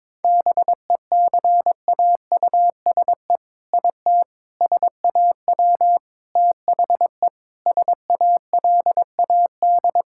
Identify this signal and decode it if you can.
Morse code, heard as a keyed tone, about 21 wpm, 700 Hz. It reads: BECAUSE IT SAW THE SALAD
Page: morse-code-ninja-practice-22wpm